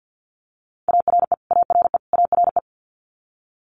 Morse code, heard as a keyed tone, about 50 wpm, 700 Hz. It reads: ARE ARE ARE
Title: Multi-letter Phonemes Repeated 3x 50wpm